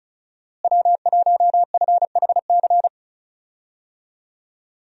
Morse code, heard as a keyed tone, about 36 wpm, 700 Hz. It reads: W1FHC
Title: USA Call Signs - 5 Characters - Repeated 3x - 35wpm